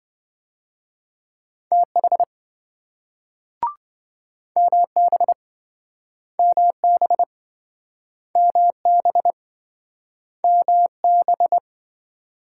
Morse code, heard as a keyed tone, about 26 wpm, 700 Hz. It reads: TH MB MB MB MB